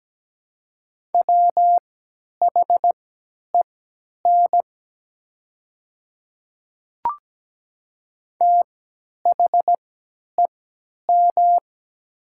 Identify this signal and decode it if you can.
Morse code, heard as a keyed tone, about 17 wpm, spaced as for 10 wpm, 700 Hz. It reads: WHEN THEM
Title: 3x Character Spacing - Top 100 Words - 17wpm